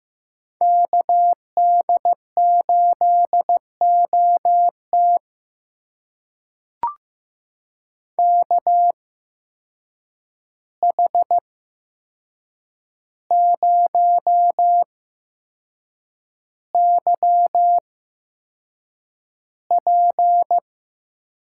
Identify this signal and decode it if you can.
Morse code, heard as a keyed tone, about 15 wpm, spaced as for 6 wpm, 700 Hz. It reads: KD8OT KH0YP
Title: USA Call Signs - 5 Characters - 8x Character Spacing - 15wpm